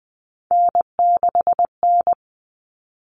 Morse code, heard as a keyed tone, about 20 wpm, 700 Hz. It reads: N6N
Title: All Cut Numbers - 3-Digit 20wpm